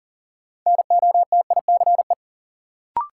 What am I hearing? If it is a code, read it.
Morse code, heard as a keyed tone, about 40 wpm, 700 Hz. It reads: NOTICE
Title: Sentences from Top 400 Words - New Word Drill - 40wpm